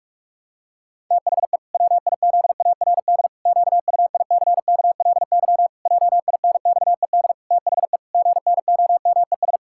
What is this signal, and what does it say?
Morse code, heard as a keyed tone, about 41 wpm, 700 Hz. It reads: THE WIZARD QUICKLY JINXED THE GNOMES
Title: Pangram Challenge v2 - 45wpm